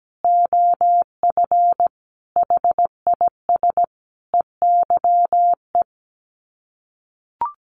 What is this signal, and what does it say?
Morse code, heard as a keyed tone, about 17 wpm, 700 Hz. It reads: OF HIS EYE
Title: Sets of 5 Words Encore2 17wpm